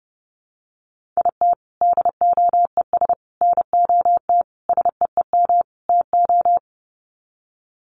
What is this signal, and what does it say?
Morse code, heard as a keyed tone, about 30 wpm, 700 Hz. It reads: IT DOES NOT SEEM TO
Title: Sets of 5 Words Encore2 30wpm